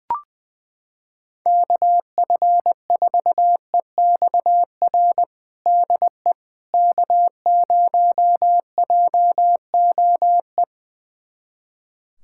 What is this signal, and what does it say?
Morse code, heard as a keyed tone, about 20 wpm, 700 Hz. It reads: KF4EXR DE K0JOE
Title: USA Callsign DE Callsign 20wpm